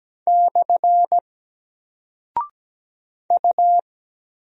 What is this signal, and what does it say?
Morse code, heard as a keyed tone, about 17 wpm, 700 Hz. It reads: / U
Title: Single Character (Letter:Number:Punctuation:Prosign) 17wpm